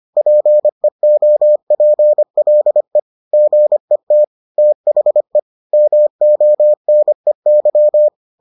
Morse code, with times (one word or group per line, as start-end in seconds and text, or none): 0.2-3.0 PEOPLE
3.3-4.2 GET
4.6-5.4 THE
5.7-8.1 MONEY